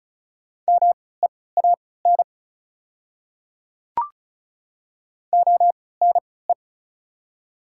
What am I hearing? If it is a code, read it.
Morse code, heard as a keyed tone, about 36 wpm, spaced as for 20 wpm, 700 Hz. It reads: MEAN ONE
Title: 3x Character Spacing - Top 500 Words - 35wpm